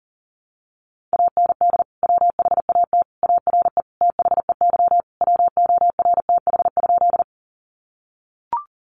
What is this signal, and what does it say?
Morse code, heard as a keyed tone, about 40 wpm, 700 Hz. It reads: AND WHAT ARE THEY WORTH?
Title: Sentences from Top 800 Words 40wpm